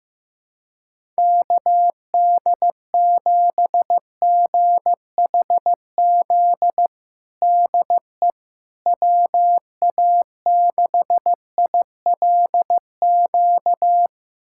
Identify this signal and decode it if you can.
Morse code, heard as a keyed tone, about 16 wpm, 700 Hz. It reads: KD7GHZ DE WA6ILQ
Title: USA Callsign DE Callsign - 6 Characters - 15wpm